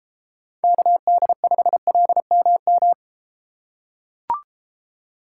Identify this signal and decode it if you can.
Morse code, heard as a keyed tone, about 33 wpm, 700 Hz. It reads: KD5LMM